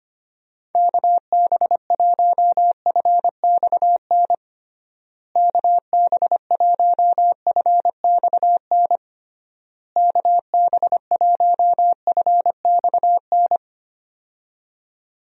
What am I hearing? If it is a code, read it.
Morse code, heard as a keyed tone, about 25 wpm, 700 Hz. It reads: KB1FXN KB1FXN KB1FXN
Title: USA Call Signs - 6 Characters - Repeated 3x - 25wpm